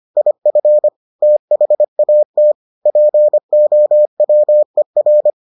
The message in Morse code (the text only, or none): IF THAT POWER